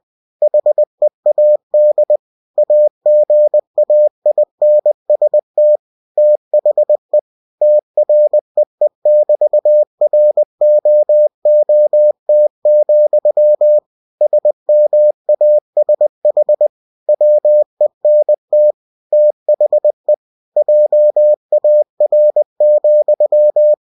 Morse code, heard as a keyed tone, about 20 wpm, 600 Hz. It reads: HEAD AGAINST THE TREE<BT>ROOT, SMASH WENT THE JAR,
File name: cw-books / winnie-the-pooh / winnie-20wpm-page-0043